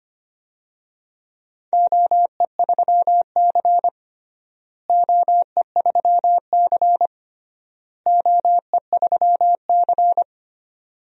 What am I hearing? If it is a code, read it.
Morse code, heard as a keyed tone, about 25 wpm, 700 Hz. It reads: OE3C OE3C OE3C
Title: DXCC-100 Call Signs - 4 Characters - Repeated 3x - 25wpm